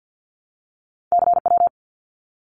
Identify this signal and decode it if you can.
Morse code, heard as a keyed tone, about 50 wpm, 700 Hz. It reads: CW
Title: Abbreviations Repeated 3x - 50wpm